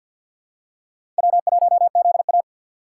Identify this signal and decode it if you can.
Morse code, heard as a keyed tone, about 46 wpm, 690 Hz. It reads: W1ZA
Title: USA Call Signs - 4 Characters - 6x Character Spacing - 50wpm